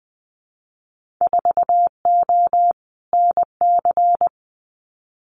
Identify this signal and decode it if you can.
Morse code, heard as a keyed tone, about 20 wpm, 700 Hz. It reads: 4O NC
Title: Winter Field Day Exchanges 20wpm